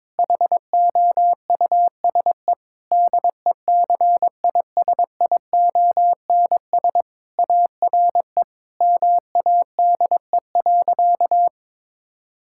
Morse code, heard as a keyed tone, about 22 wpm, 700 Hz. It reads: HOUSE DECISIONS ARE MADE.
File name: practice / 22wpm / Sentences with 4-Word Phrases - 22wpm